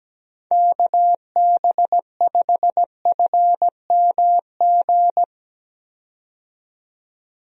Morse code, heard as a keyed tone, about 17 wpm, 700 Hz. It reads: KB5FMG
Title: USA Call Signs - 6 Characters - 17wpm